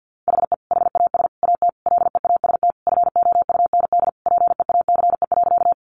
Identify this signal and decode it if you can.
Morse code, heard as a keyed tone, about 50 wpm, 700 Hz. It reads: HE HAS AT LEAST FOUND PEACE.
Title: Sentences from Top 1500 Words 50wpm